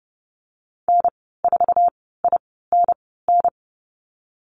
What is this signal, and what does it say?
Morse code, heard as a keyed tone, about 30 wpm, spaced as for 17 wpm, 700 Hz. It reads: N4INN
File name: Lesson 14g - Number 2 - Call Signs 30wpm